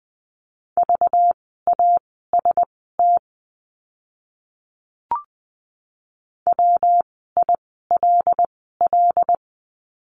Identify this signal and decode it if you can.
Morse code, heard as a keyed tone, about 20 wpm, spaced as for 15 wpm, 700 Hz. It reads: VAST WILL
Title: 192 Lesson 31f - Letter v - Words 20wpm